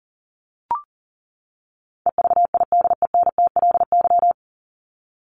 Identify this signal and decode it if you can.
Morse code, heard as a keyed tone, about 40 wpm, 700 Hz. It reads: EVIDENTLY
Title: LY Suffix - Top 100 Words with Base Words 40wpm